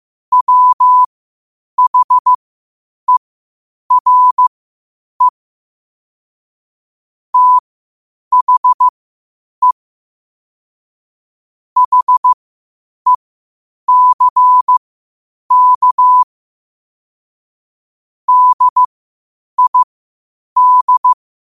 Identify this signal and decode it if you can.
Morse code, heard as a keyed tone, about 15 wpm, spaced as for 8 wpm, 1000 Hz. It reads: WHERE THE HECK DID
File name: qotdfiles / 8wpm